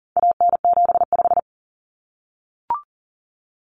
Morse code, heard as a keyed tone, about 40 wpm, 700 Hz. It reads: AN75
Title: Common Cut Numbers - 4-Digit 40wpm